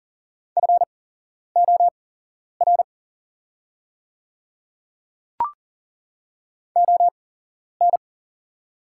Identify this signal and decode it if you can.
Morse code, heard as a keyed tone, about 40 wpm, spaced as for 11 wpm, 700 Hz. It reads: FOR ON
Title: 8x Character Spacing - Top 100 Words - 40wpm